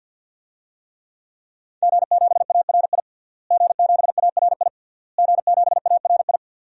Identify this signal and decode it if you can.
Morse code, heard as a keyed tone, about 48 wpm, 700 Hz. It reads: G7ARI G7ARI G7ARI
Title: DXCC-100 Call Signs - 5 Characters - Repeated 3x - 50wpm